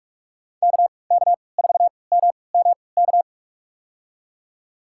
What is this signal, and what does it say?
Morse code, heard as a keyed tone, about 42 wpm, spaced as for 25 wpm, 700 Hz. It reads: KK4MMK
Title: USA Call Signs - 6 Characters - 3x Character Spacing - 45wpm